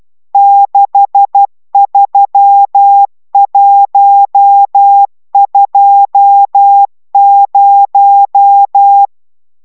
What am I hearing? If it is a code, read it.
Morse code, heard as a keyed tone, about 12 wpm, 800 Hz. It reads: 63120